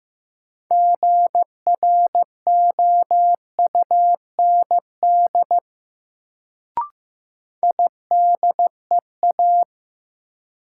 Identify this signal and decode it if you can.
Morse code, heard as a keyed tone, about 16 wpm, 700 Hz. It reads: GROUND IDEA
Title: Sets of 3 Words - New Word Drill - 15wpm